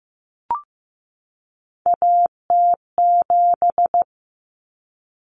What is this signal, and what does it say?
Morse code, heard as a keyed tone, about 15 wpm, 700 Hz. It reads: AT7
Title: Common Cut Numbers - 3-Digit 15wpm